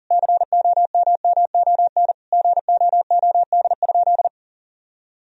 Morse code, heard as a keyed tone, about 40 wpm, 700 Hz. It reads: COMMON GOOD?
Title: Sentences from Top 1000 Words - Review Part 3 - 40wpm